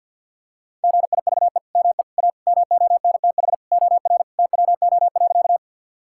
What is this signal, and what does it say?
Morse code, heard as a keyed tone, about 52 wpm, 700 Hz. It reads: GIVE ME A MONTH OR TWO.
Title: Sentences from Top 1000 Words - Review Part 5 - 50wpm